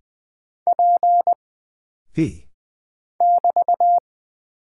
Morse code, heard as a keyed tone, about 20 wpm, 700 Hz. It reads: P <BT>
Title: Single Character (Letter:Number:Punctuation:Prosign) - Mind-Melt - 20wpm